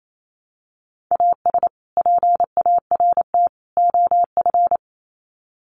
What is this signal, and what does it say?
Morse code, heard as a keyed tone, about 28 wpm, 700 Hz. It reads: AS PART OF
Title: Sets of 3 Words 28wpm